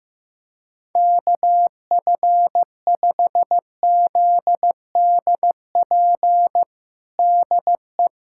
Morse code, heard as a keyed tone, about 15 wpm, 700 Hz. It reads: KF5ZDP DE
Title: USA Callsign DE Callsign 15wpm